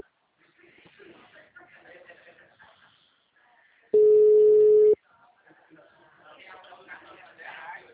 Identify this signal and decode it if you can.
Ringback tone (Europe, 425 Hz), with speech.